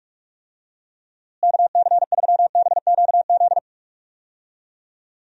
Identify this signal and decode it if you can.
Morse code, heard as a keyed tone, about 44 wpm, 700 Hz. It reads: KC3BQZ
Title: USA Call Signs - 6 Characters - Repeated 2x - 45wpm